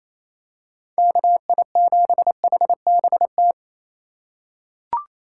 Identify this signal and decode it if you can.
Morse code, heard as a keyed tone, about 28 wpm, 700 Hz. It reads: KI7HBT
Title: USA Call Signs - 6 Characters - 3x Character Spacing - 28wpm